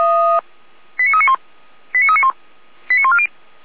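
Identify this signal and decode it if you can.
Five-tone selective call (ZVEI), heard as quick runs of five tones, tones of about 80 ms. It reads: 89391 89391 89140
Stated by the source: Received by SDR, ZVEI five-tone selective call 89187.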